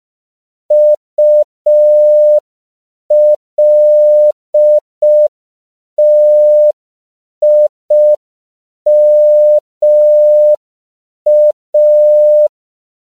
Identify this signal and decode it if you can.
Morse code, heard as a keyed tone, about 5 wpm, 600 Hz. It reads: ULTIMA
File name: CW / 5PPM